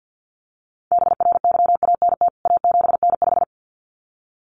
Morse code, heard as a keyed tone, about 50 wpm, 700 Hz. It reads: BRYANT A7N5
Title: CWT Exchange 50wpm